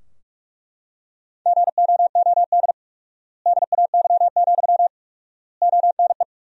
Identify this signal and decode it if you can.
Morse code, heard as a keyed tone, about 41 wpm, 700 Hz. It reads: GOOD DAY, ONE
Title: 4x Word-Spacing - Sentences from Top 200 Words - 45wpm